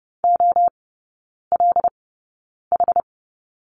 Morse code, heard as a keyed tone, about 30 wpm, spaced as for 9 wpm, 700 Hz. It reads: OLH
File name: 081 Lesson 14e - Number 2 - Three Characters 30wpm